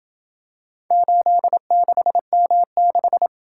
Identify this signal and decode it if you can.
Morse code, heard as a keyed tone, about 27 wpm, 700 Hz. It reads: 86M6